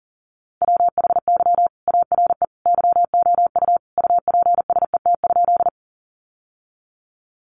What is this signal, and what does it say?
Morse code, heard as a keyed tone, about 40 wpm, 700 Hz. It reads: WHY ARE YOU UPSET?
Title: Sentences from Top 800 Words 40wpm